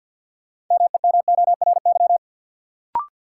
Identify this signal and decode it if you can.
Morse code, heard as a keyed tone, about 52 wpm, 700 Hz. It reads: MEMORY
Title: Sentences from Top 1300 Words - New Word Drill - 50wpm